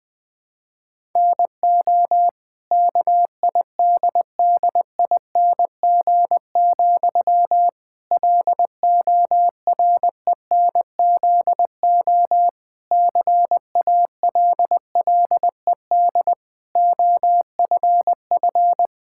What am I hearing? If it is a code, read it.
Morse code, heard as a keyed tone, about 21 wpm, 700 Hz. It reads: NO KIDDING, LORENZO CALLED OFF
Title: Pangram Challenge v2 - 20wpm